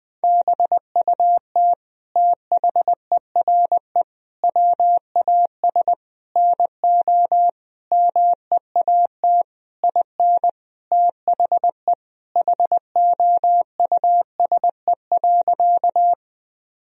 Morse code, heard as a keyed tone, about 20 wpm, 700 Hz. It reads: BUT THERE WAS NO MEAT IN THE HOUSE.
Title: Sentences from Top 1000 Words - Review Part 3 - 20wpm